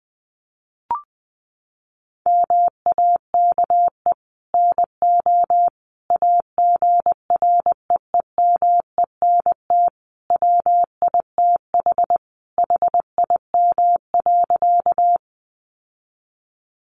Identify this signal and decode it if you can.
Morse code, heard as a keyed tone, about 19 wpm, 700 Hz. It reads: MAKE NO AGREEMENT WITH HIM.
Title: Sentences from Top 1500 Words 20wpm